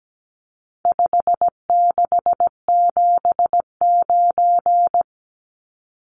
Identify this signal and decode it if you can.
Morse code, heard as a keyed tone, about 17 wpm, 700 Hz. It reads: 5679
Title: Numbers - 4-Digit 17wpm